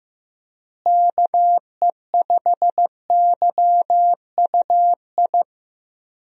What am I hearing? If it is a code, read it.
Morse code, heard as a keyed tone, about 15 wpm, 700 Hz. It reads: KE5YUI
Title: USA Call Signs - 6 Characters - Repeated 2x - 15wpm